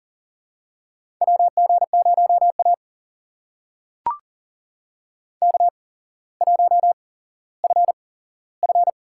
Morse code, heard as a keyed tone, about 40 wpm, spaced as for 17 wpm, 700 Hz. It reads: WG0A K1FF
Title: USA Call Signs - 4 Characters - 8x Character Spacing - 40wpm